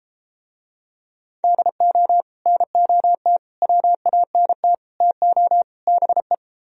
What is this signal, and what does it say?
Morse code, heard as a keyed tone, about 33 wpm, 700 Hz. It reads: DO NOT WANT TO BE